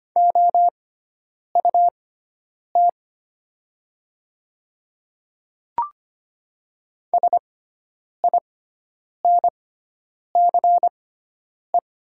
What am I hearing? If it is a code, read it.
Morse code, heard as a keyed tone, about 25 wpm, spaced as for 9 wpm, 700 Hz. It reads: OUT SINCE
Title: 6x Character Spacing - Top 300 Words - 25wpm